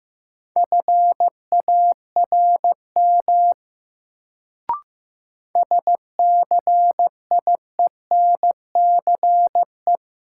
Morse code, heard as a keyed tone, about 15 wpm, 700 Hz. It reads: FARM SCIENCE
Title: Sentences from Top 1000 Words - Review Part 2 - New Word Drill - 15wpm